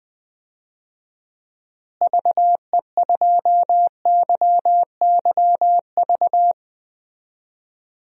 Morse code, heard as a keyed tone, about 20 wpm, 700 Hz. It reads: VE2YYV
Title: DXCC-100 Call Signs - 6 Characters - Repeated 2x - 20wpm